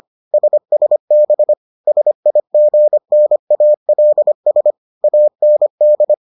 Morse code, heard as a keyed tone, about 25 wpm, 600 Hz. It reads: SSB SIGNALS AND